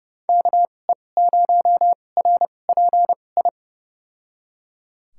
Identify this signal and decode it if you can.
Morse code, heard as a keyed tone, about 31 wpm, spaced as for 22 wpm, 700 Hz. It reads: KE0RPI
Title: USA Call Signs - 6 Characters - 2x Character Spacing - 30wpm